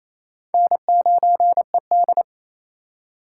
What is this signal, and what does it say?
Morse code, heard as a keyed tone, about 28 wpm, 700 Hz. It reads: N9ED
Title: USA Call Signs - 2x Character Spacing - 28wpm